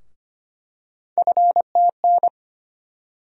Morse code, heard as a keyed tone, about 25 wpm, 700 Hz. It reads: FTN
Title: USA Call Sign Postfixes - 3 Letters 25wpm